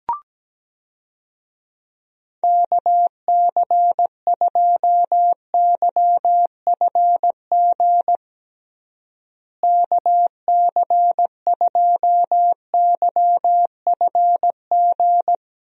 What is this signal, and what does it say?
Morse code, heard as a keyed tone, about 17 wpm, 700 Hz. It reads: KC2YFG KC2YFG
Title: USA Call Signs - 6 Characters - Repeated 2x - 17wpm